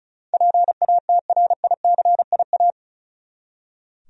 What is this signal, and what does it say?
Morse code, heard as a keyed tone, about 36 wpm, 700 Hz. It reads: PATRICIA
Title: USA Common Names - 35wpm